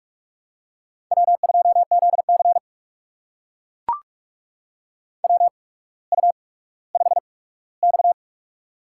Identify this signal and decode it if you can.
Morse code, heard as a keyed tone, about 44 wpm, spaced as for 19 wpm, 690 Hz. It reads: W2ZC WU5X
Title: USA Call Signs - 4 Characters - 8x Character Spacing - 45wpm